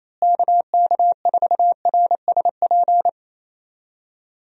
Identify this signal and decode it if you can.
Morse code, heard as a keyed tone, about 28 wpm, 700 Hz. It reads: KK4RSP
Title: USA Call Signs - 6 Characters - Repeated 2x - 28wpm